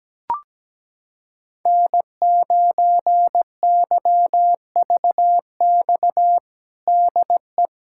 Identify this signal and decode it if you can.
Morse code, heard as a keyed tone, about 17 wpm, 700 Hz. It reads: N9YVX DE
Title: USA Callsign DE Callsign - 5 Characters - 17wpm